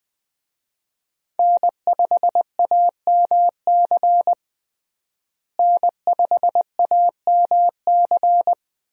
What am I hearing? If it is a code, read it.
Morse code, heard as a keyed tone, about 20 wpm, 700 Hz. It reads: N5AMC N5AMC
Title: USA Call Signs - 5 Characters - Repeated 2x - 20wpm